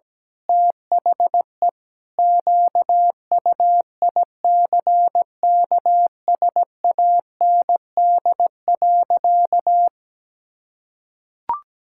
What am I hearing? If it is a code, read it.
Morse code, heard as a keyed tone, about 17 wpm, 700 Hz. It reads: THE QUICKSAND.
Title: Pangram Challenge v2 - 17wpm